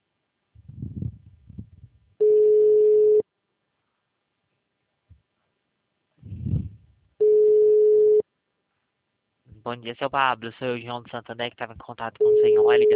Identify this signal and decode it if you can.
Ringback tone (Europe, 425 Hz), with speech.